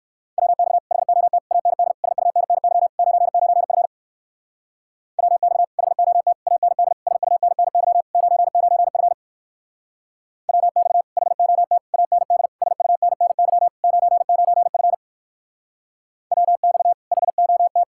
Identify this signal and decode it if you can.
Morse code, heard as a keyed tone, about 62 wpm, 690 Hz. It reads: WX HOT AND SUNNY 99F WX HOT AND SUNNY 99F WX HOT AND SUNNY 99F WX HOT